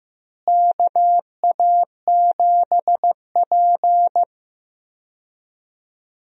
Morse code, heard as a keyed tone, about 15 wpm, 700 Hz. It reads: KA7P
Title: USA Call Signs - 4 Characters - Repeated 3x - 15wpm